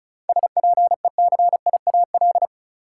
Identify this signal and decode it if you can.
Morse code, heard as a keyed tone, about 36 wpm, 700 Hz. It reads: SPECIAL
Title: Sentences from Top 1000 Words - Review Part 3 - New Word Drill - 35wpm